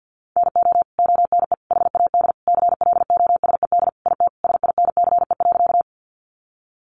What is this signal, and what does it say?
Morse code, heard as a keyed tone, about 50 wpm, 700 Hz. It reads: NO ONE HAD CLOSED IT SINCE.
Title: Sentences from Top 1400 Words 50wpm